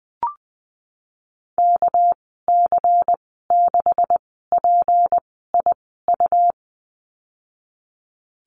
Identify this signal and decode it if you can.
Morse code, heard as a keyed tone, about 19 wpm, spaced as for 14 wpm, 700 Hz. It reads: KC6PIU
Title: 238 Lesson 38g - Letter q - Call Signs 20wpm